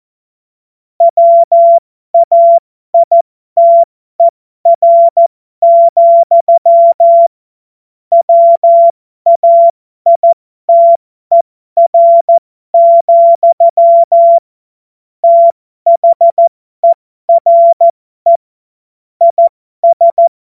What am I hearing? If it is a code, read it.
Morse code, heard as a keyed tone, about 14 wpm, spaced as for 12 wpm, 680 Hz. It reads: WAITER, WAITER, THERE IS
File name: Morsekurs / Witze